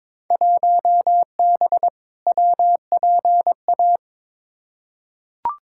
Morse code, heard as a keyed tone, about 22 wpm, 700 Hz. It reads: 1B WPA
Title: Field Day Exchanges Encore - 22wpm